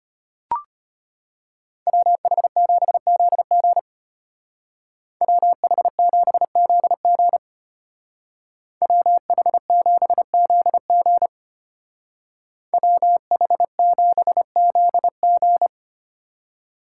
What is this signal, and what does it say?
Morse code, heard as a keyed tone, about 33 wpm, 700 Hz. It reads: WH7ZG WH7ZG WH7ZG WH7ZG